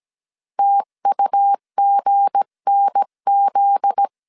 Morse code, heard as a keyed tone, about 17 wpm, 800 Hz. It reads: TUGNZ
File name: CW / DET / 1 9 1500C S B 16wpm 2856